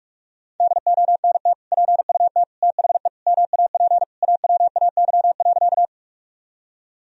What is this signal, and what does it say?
Morse code, heard as a keyed tone, about 41 wpm, 700 Hz. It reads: DONT PUT THE MAP AWAY.
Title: Sentences from Top 1000 Words - Review Part 2 - 45wpm